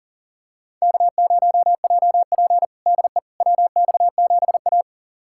Morse code, heard as a keyed tone, about 40 wpm, 700 Hz. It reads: K0JP DE WX7A
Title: USA Callsign DE Callsign - 4 Characters - 40wpm